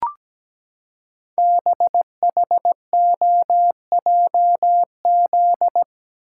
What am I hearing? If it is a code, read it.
Morse code, heard as a keyed tone, about 17 wpm, 700 Hz. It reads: BHOJZ
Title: Code Groups - 5 Letters - 17wpm